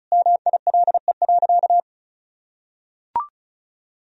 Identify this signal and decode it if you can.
Morse code, heard as a keyed tone, about 36 wpm, 700 Hz. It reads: MILE.
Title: Sentences from Top 1000 Words - Review Part 1 - 35wpm